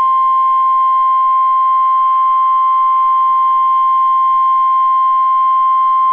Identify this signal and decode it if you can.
Emergency Alert System (SAME) tones (the 1050 Hz attention tone). It is heard from 0.4 to 5.4 s.